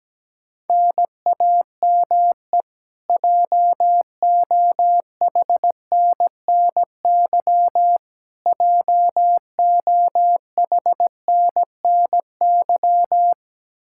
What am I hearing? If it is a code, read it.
Morse code, heard as a keyed tone, about 17 wpm, 700 Hz. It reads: NAME JOHNNY JOHNNY
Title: Practice QSO - Exchange 1 - Beginning - 17wpm